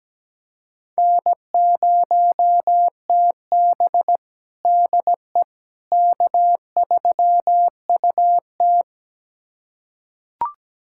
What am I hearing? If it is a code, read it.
Morse code, heard as a keyed tone, about 17 wpm, 700 Hz. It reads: N0TB DE K3UT
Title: USA Callsign DE Callsign - 4 Characters - 17wpm